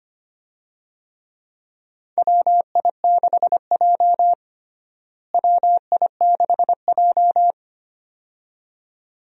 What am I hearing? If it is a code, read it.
Morse code, heard as a keyed tone, about 25 wpm, 700 Hz. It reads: WI6J WI6J
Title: USA Call Signs - 4 Characters - Repeated 2x - 25wpm